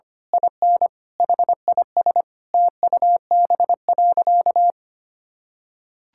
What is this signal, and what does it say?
Morse code, heard as a keyed tone, about 25 wpm, 700 Hz. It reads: IN HIS TUB.
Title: Pangram Challenge v2 - 25wpm